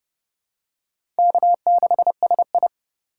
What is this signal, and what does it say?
Morse code, heard as a keyed tone, about 30 wpm, 700 Hz. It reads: K6SI
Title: USA Call Signs - 4 Characters - 4x Character Spacing - 30wpm